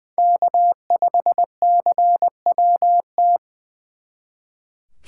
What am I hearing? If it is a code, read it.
Morse code, heard as a keyed tone, about 23 wpm, 700 Hz. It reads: K5CWT
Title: Speed Racing - USA Call Signs 20wpm